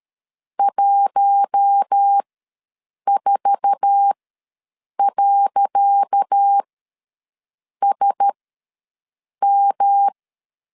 Morse code, heard as a keyed tone, about 13 wpm, spaced as for 6 wpm, 800 Hz. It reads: 14.SM
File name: LEZIONI / 10 SIGNS / 10C DOT ws 12wpm 3548